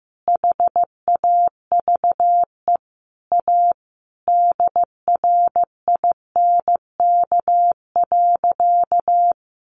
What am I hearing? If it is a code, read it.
Morse code, heard as a keyed tone, about 15 wpm, 700 Hz. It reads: HAVE A DRINK.
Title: Sentences from Top 1500 Words 15wpm